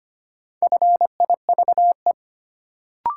Morse code, heard as a keyed tone, about 25 wpm, 700 Hz. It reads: FIVE
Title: Sentences from Top 400 Words - New Word Drill - 25wpm